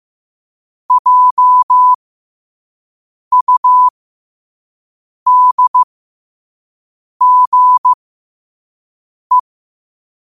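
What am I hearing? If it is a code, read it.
Morse code, heard as a keyed tone, about 15 wpm, spaced as for 5 wpm, 1000 Hz. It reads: JUDGE
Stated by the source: Morse code quote-of-the-day practice at 5 wpm.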